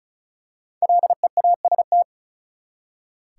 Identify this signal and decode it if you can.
Morse code, heard as a keyed tone, about 36 wpm, 700 Hz. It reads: LEAST
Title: Five Letter Words - 35wpm